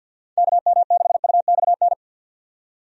Morse code, heard as a keyed tone, about 51 wpm, 700 Hz. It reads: KM6UXN
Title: USA Call Signs - 6 Characters - 50wpm